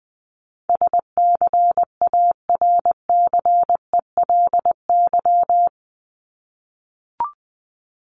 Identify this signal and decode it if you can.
Morse code, heard as a keyed tone, about 20 wpm, 700 Hz. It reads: SCARCELY
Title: LY Suffix - Top 100 Words with Base Words 20wpm